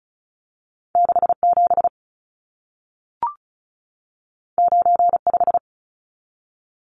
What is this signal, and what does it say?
Morse code, heard as a keyed tone, about 36 wpm, 700 Hz. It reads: 67 95
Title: Numbers - 2-Digit 35wpm